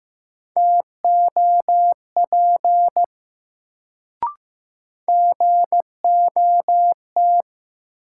Morse code, heard as a keyed tone, about 15 wpm, 700 Hz. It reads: TOP GOT
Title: Sets of 2 Words - New Word Drill - 15wpm